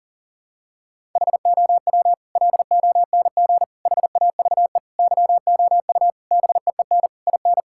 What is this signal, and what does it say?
Morse code, heard as a keyed tone, about 40 wpm, 700 Hz. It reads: HOW LONG HAVE YOU BEEN IN
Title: Sentences from Top 1000 Words - Review Part 6 - 40wpm